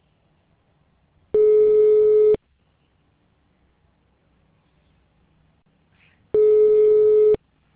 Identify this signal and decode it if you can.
Ringback tone (Europe, 425 Hz).